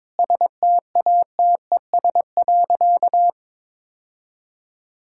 Morse code, heard as a keyed tone, about 21 wpm, 700 Hz. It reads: STATES.
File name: practice / 22wpm / Sentences with 3-Word Phrases - 22wpm